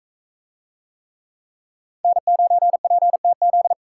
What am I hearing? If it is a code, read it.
Morse code, heard as a keyed tone, about 40 wpm, 700 Hz. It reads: N9PTZ